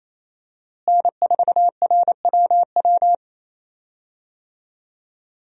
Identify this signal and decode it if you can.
Morse code, heard as a keyed tone, about 28 wpm, 700 Hz. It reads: N4RWW
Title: USA Call Signs - 5 Characters - Repeated 2x - 28wpm